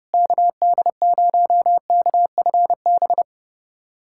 Morse code, heard as a keyed tone, about 32 wpm, 700 Hz. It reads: KD0KFB
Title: USA Callsign DE Callsign - 6 Characters - 30wpm